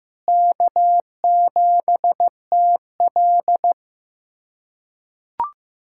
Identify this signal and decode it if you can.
Morse code, heard as a keyed tone, about 15 wpm, 700 Hz. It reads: K7TL
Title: USA Callsign DE Callsign - 4 Characters - 15wpm